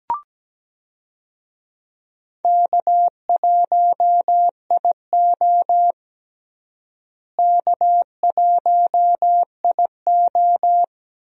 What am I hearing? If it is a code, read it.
Morse code, heard as a keyed tone, about 17 wpm, 700 Hz. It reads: K1IO K1IO
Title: USA Call Signs - 4 Characters - Repeated 2x - 17wpm